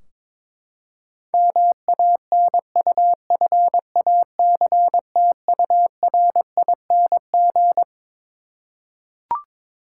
Morse code, heard as a keyed tone, about 22 wpm, 700 Hz. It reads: MANUFACTURING
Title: Practice QSO - Exchange 3 - More Info - 22wpm